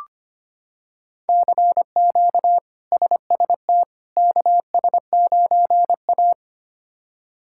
Morse code, heard as a keyed tone, about 25 wpm, 700 Hz. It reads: CQ SST KS9A
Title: K1USN SST QSO Practice - 25wpm